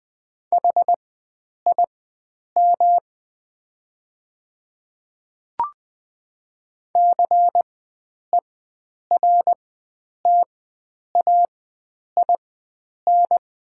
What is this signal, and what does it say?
Morse code, heard as a keyed tone, about 20 wpm, spaced as for 9 wpm, 700 Hz. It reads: HIM CERTAIN
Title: 4x Character Spacing - Top 500 Words - 20wpm